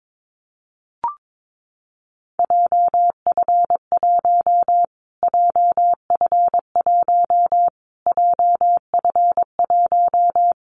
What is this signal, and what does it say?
Morse code, heard as a keyed tone, about 22 wpm, 700 Hz. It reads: JF1 JF1 JF1
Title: Callsign Prefixes for Top 25 DXCC - Repeated 3x - 22wpm